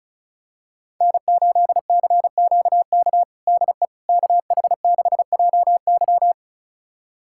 Morse code, heard as a keyed tone, about 34 wpm, 700 Hz. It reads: N8CQK DE KH6JY
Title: USA Callsign DE Callsign - 5 Characters - 35wpm